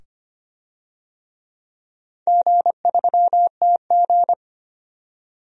Morse code, heard as a keyed tone, about 25 wpm, 700 Hz. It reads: G3TG
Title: DXCC-100 Call Signs - 4 Characters - 25wpm